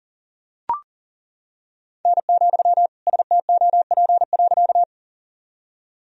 Morse code, heard as a keyed tone, about 38 wpm, 700 Hz. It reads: N, STOP.
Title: Sentences from Top 300 Words and Abbreviations 40wpm